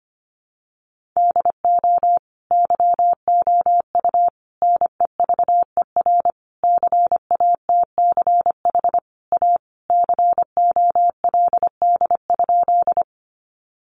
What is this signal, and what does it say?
Morse code, heard as a keyed tone, about 25 wpm, 700 Hz. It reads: DO YOU NEVER CATCH A COLD?
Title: Sentences from Top 1500 Words 25wpm